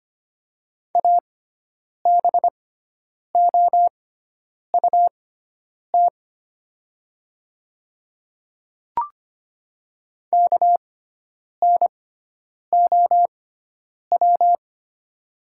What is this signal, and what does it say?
Morse code, heard as a keyed tone, about 25 wpm, spaced as for 9 wpm, 700 Hz. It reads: ABOUT KNOW
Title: 6x Character Spacing - Top 100 Words - 25wpm